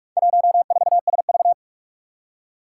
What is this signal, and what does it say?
Morse code, heard as a keyed tone, about 46 wpm, 700 Hz. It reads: 14SV
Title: Four Challenging Letter-Number Combinations 45wpm